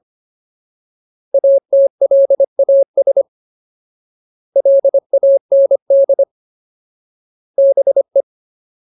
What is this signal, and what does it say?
Morse code, heard as a keyed tone, about 25 wpm, 550 Hz. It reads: ATLAS LAND BE
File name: Ultimate Word List / 25WPM Ultimate